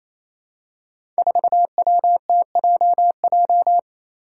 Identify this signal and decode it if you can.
Morse code, heard as a keyed tone, about 28 wpm, 700 Hz. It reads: 4WTJJ
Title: Five Challenging Letter-Number Combinations 28wpm